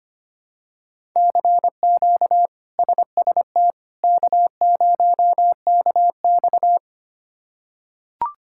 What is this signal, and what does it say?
Morse code, heard as a keyed tone, about 25 wpm, 700 Hz. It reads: CQ SST K0KX
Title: K1USN SST QSO Practice - 25wpm